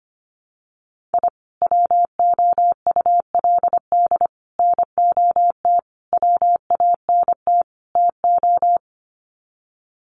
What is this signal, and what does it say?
Morse code, heard as a keyed tone, about 25 wpm, 700 Hz. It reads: I WOULD NOT WANT TO
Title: Sets of 5 Words 25wpm